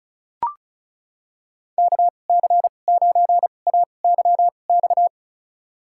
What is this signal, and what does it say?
Morse code, heard as a keyed tone, about 34 wpm, spaced as for 25 wpm, 700 Hz. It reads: KC9AYX
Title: USA Call Signs - 6 Characters - 2x Character Spacing - 35wpm